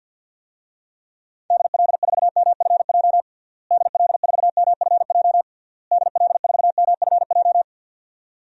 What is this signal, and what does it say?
Morse code, heard as a keyed tone, about 52 wpm, 700 Hz. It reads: DL4MFJ DL4MFJ DL4MFJ
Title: DXCC-100 Call Signs - 6 Characters - Repeated 3x - 50wpm